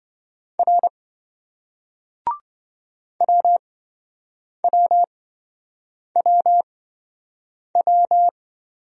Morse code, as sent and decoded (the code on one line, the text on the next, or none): .-. / .-- .-- .-- .--
R WWWW